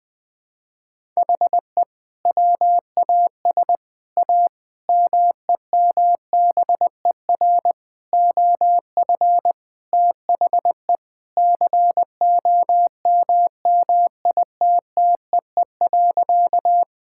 Morse code, heard as a keyed tone, about 20 wpm, 700 Hz. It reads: HE WAS A MEMBER OF THE COMMITTEE.
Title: Sentences with 6-Word Phrases - 20wpm